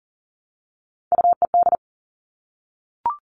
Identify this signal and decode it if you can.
Morse code, heard as a keyed tone, about 40 wpm, 700 Hz. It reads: UED
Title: All Cut Numbers - 3-Digit 40wpm